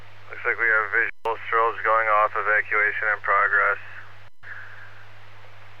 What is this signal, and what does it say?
Emergency Alert System (SAME) tones (header bursts), heard as harsh buzzing bursts, with speech. It is heard from 0.3 to 5.1 s.